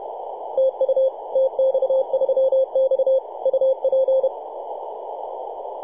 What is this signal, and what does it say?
Morse code, heard as a keyed tone, about 33 wpm, 530 Hz, with radio noise. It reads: TU TX3X UP
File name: posts / 2015-oct-03-2319z-tx3x-21-023mhz-cw